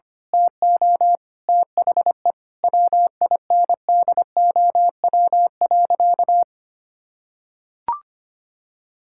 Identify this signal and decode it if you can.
Morse code, heard as a keyed tone, about 25 wpm, 700 Hz. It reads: TO THE WINDOW.
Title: Sentences from Top 1600 Words 25wpm